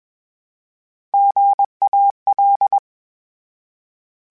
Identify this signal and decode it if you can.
Morse code, heard as a keyed tone, about 21 wpm, 800 Hz. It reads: GAL
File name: Misc / Misc-mp3s / Misc @20